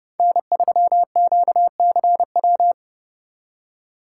Morse code, heard as a keyed tone, about 29 wpm, 700 Hz. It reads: N3QCW